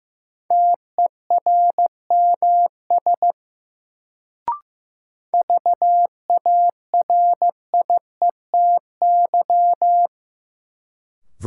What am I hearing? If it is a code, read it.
Morse code, heard as a keyed tone, about 16 wpm, 700 Hz. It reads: TERMS VARIETY
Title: Sets of 3 Words - New Word Drill - 15wpm